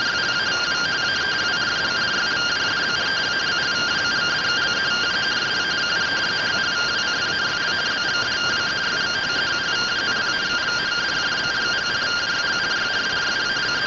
Two-tone teleprinter signal, about 170 Hz shift.
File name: findings / 100 surview 0.gen-11 rtty rx